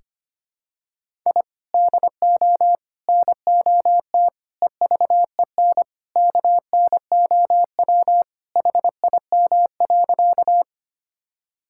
Morse code, heard as a keyed tone, about 25 wpm, 700 Hz. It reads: I DO NOT EVEN KNOW HIM.
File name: practice / 25wpm / Sentences with 4-Word Phrases - 25wpm